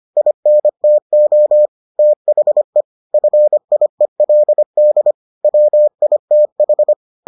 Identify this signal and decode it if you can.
Morse code, heard as a keyed tone, about 25 wpm, 600 Hz. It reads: INTO THE FIELD WITH